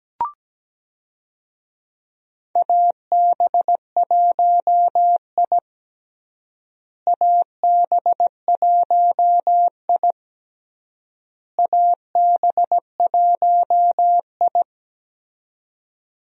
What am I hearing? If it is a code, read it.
Morse code, heard as a keyed tone, about 17 wpm, 700 Hz. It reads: AB1I AB1I AB1I
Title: USA Call Signs - 4 Characters - Repeated 3x - 17wpm